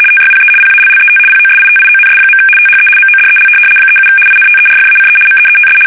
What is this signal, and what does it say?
RTTY, about 75 baud. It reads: TYPE PALLET POSITIONS. THE MODEL 37 INTERFACE MEETS
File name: fsk850-75B / teletype-corp